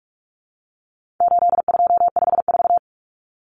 Morse code, heard as a keyed tone, about 44 wpm, 700 Hz. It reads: 8254
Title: Common Cut Numbers - 4-Digit 45wpm